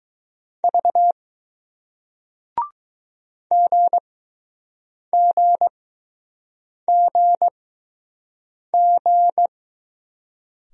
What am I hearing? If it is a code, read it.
Morse code, heard as a keyed tone, about 20 wpm, spaced as for 6 wpm, 700 Hz. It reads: V GGGG